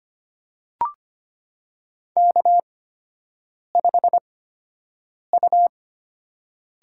Morse code, heard as a keyed tone, about 25 wpm, spaced as for 7 wpm, 700 Hz. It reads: K5U
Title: USA Call Signs - 4 Characters - 8x Character Spacing - 25wpm